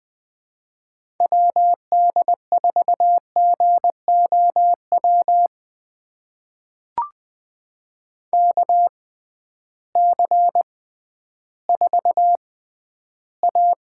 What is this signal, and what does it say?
Morse code, heard as a keyed tone, about 20 wpm, spaced as for 12 wpm, 700 Hz. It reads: WD4GOW KC4A
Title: USA Call Signs - 6 Characters - 6x Character Spacing - 20wpm